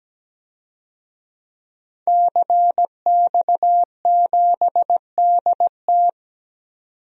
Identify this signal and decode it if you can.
Morse code, heard as a keyed tone, about 17 wpm, 700 Hz. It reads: CX7DT
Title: DXCC-100 Call Signs - 5 Characters - 17wpm